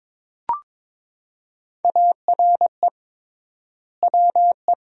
Morse code, heard as a keyed tone, about 22 wpm, 700 Hz. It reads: ARE WE
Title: 3x Word-Spacing - Sentences from Top 400 Words - 22wpm